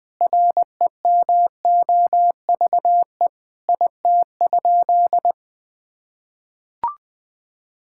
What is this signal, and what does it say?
Morse code, heard as a keyed tone, about 20 wpm, 700 Hz. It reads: REMOVE IT?
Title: Sentences from Top 1000 Words - Review Part 5 - 20wpm